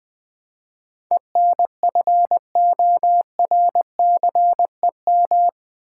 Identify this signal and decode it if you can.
Morse code, heard as a keyed tone, about 20 wpm, 700 Hz. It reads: ENFORCEM
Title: Build A Word - 4 Letters At A Time - 11-Letter Words - 20wpm